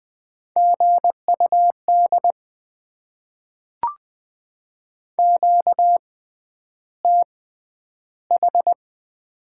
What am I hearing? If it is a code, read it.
Morse code, heard as a keyed tone, about 20 wpm, spaced as for 10 wpm, 700 Hz. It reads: GUD QTH
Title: Top 20 QSO Elements - 6x Character Spacing - 20wpm